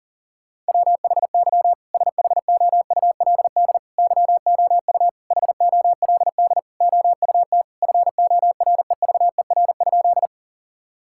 Morse code, heard as a keyed tone, about 37 wpm, 700 Hz. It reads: WHY SHOULD YOU HOLD OUT FOREVER?
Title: Sentences from Top 1600 Words 40wpm